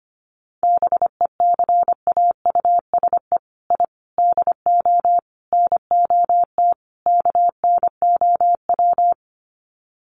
Morse code, heard as a keyed tone, about 25 wpm, 700 Hz. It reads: BECAUSE I DO NOT KNOW
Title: Sets of 5 Words Encore2 25wpm